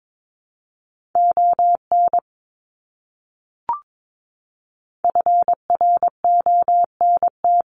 Morse code, heard as a keyed tone, about 22 wpm, 700 Hz. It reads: ON FRONT
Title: Top 500 Words Repeated 22wpm